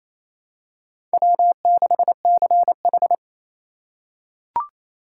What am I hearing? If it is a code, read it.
Morse code, heard as a keyed tone, about 28 wpm, 700 Hz. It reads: W6CH
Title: USA Call Signs - 4x Character Spacing - 28wpm